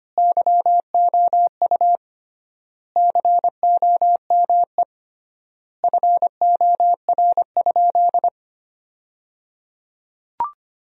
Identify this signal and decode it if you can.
Morse code, heard as a keyed tone, about 25 wpm, 700 Hz. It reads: YOU COME FOR?
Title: 3x Word-Spacing - Sentences from Top 100 Words - 25wpm